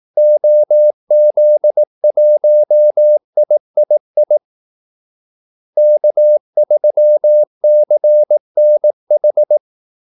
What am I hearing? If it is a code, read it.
Morse code, heard as a keyed tone, about 18 wpm, 600 Hz. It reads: OZ1III K3CNH
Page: Kaldesignaler 18 wpm